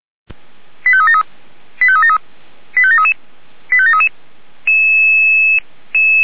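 Five-tone selective call (ZVEI), heard as quick runs of five tones, tones of about 70 ms. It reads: 85373 85373 85733 85733
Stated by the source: ZVEI five-tone selective call 89482, received by SDR.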